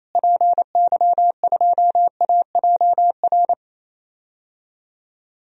Morse code, heard as a keyed tone, about 27 wpm, 700 Hz. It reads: PY2AJR
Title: DXCC-100 Call Signs - 6 Characters - Repeated 3x - 28wpm